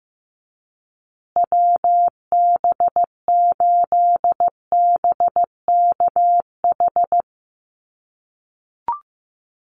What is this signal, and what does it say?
Morse code, heard as a keyed tone, about 15 wpm, 700 Hz. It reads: WB8BKH
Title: USA Call Signs Repeated 3x 15wpm